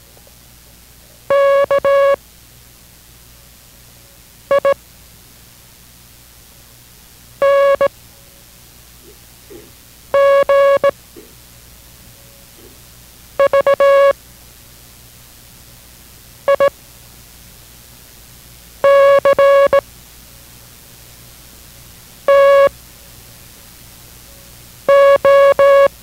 Morse code, heard as a keyed tone, about 17 wpm, spaced as for 4 wpm, 560 Hz. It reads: KING VICTO